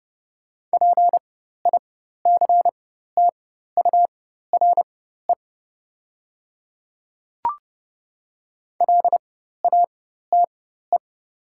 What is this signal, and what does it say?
Morse code, heard as a keyed tone, about 30 wpm, spaced as for 14 wpm, 700 Hz. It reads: PICTURE LATE
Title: 4x Character Spacing - Top 300 Words - 30wpm